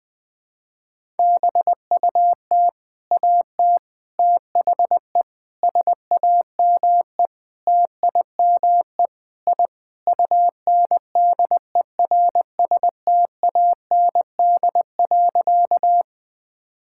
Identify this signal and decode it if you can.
Morse code, heard as a keyed tone, about 21 wpm, 700 Hz. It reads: BUT AT THE SAME TIME I UNDERSTAND.
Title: Sentences with 4-Word Phrases - 20wpm